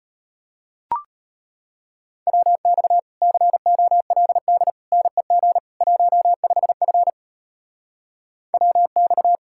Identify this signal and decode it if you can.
Morse code, heard as a keyed tone, about 32 wpm, 700 Hz. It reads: WX COLD NEG 15F WX